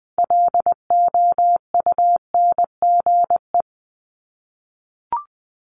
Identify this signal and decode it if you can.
Morse code, heard as a keyed tone, about 19 wpm, 700 Hz. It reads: LOUNGE
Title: Multi-letter Phonemes 20wpm